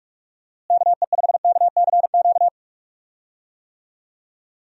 Morse code, heard as a keyed tone, about 46 wpm, 700 Hz. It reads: KE5KCQ
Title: USA Call Signs - 6 Characters - Repeated 2x - 45wpm